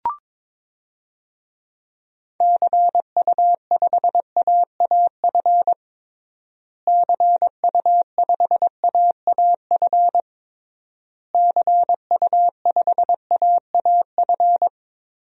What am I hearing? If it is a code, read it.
Morse code, heard as a keyed tone, about 22 wpm, 700 Hz. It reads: CU5AAF CU5AAF CU5AAF
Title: DXCC-100 Call Signs - 6 Characters - Repeated 3x - 22wpm